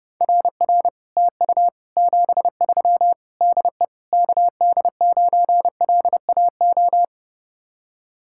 Morse code, heard as a keyed tone, about 30 wpm, 700 Hz. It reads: RR TU 73 DE KD9LAO